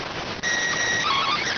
SSTV (slow-scan television), PD 290.